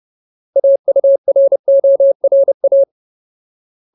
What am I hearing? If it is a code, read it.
Morse code, heard as a keyed tone, about 30 wpm, 550 Hz. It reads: AURORA
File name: Ultimate Word List / 30WPM Ultimate